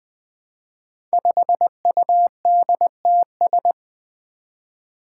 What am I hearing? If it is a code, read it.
Morse code, heard as a keyed tone, about 20 wpm, 700 Hz. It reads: 5UDTS
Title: Five Challenging Letter-Number Combinations 20wpm